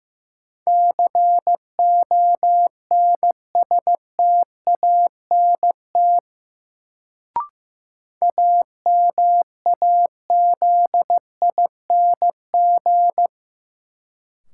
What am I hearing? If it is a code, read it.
Morse code, heard as a keyed tone, about 15 wpm, 700 Hz. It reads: CONSTANT AMAZING
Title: Sentences from Top 1500 Words - New Word Drill Alt - 15wpm